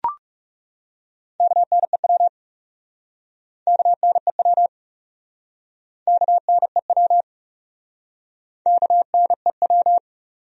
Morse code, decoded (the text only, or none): KNEW KNEW KNEW KNEW